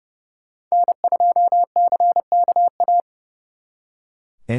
Morse code, heard as a keyed tone, about 35 wpm, 700 Hz. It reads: N2CKA
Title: Speed Racing - USA Call Signs 30wpm